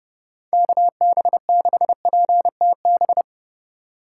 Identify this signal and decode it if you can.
Morse code, heard as a keyed tone, about 37 wpm, 700 Hz. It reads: KB6PTB
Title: Speed Racing - USA 2x3 Call Signs 30wpm